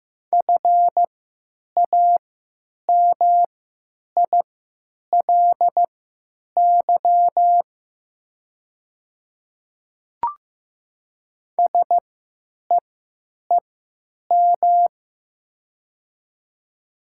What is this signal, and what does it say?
Morse code, heard as a keyed tone, about 15 wpm, spaced as for 9 wpm, 700 Hz. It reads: FAMILY SEEM
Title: 3x Character Spacing - Top 500 Words - 15wpm